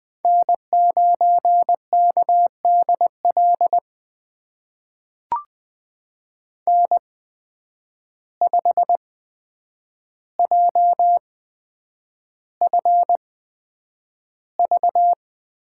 Morse code, heard as a keyed tone, about 21 wpm, spaced as for 9 wpm, 700 Hz. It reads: N9KDL N5JFV
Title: USA Call Signs - 5 Characters - 8x Character Spacing - 20wpm